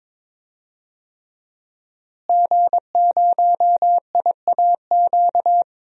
Morse code, heard as a keyed tone, about 22 wpm, 700 Hz. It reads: G0IAQ
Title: DXCC-100 Call Signs - 5 Characters - Repeated 3x - 22wpm